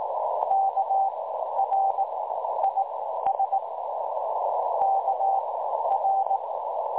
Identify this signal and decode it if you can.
Morse code, heard as a keyed tone, about 19 wpm, spaced as for 12 wpm, 780 Hz, with noise. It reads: KR IS KR
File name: dlazaawansowanych / nagraniazpasma / websdr recording 3565.0kHz